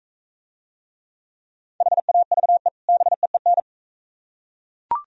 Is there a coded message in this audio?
Yes: Morse code from 1.8 to 3.6 s, heard as a keyed tone, about 44 wpm, 700 Hz. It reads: HAVE BEEN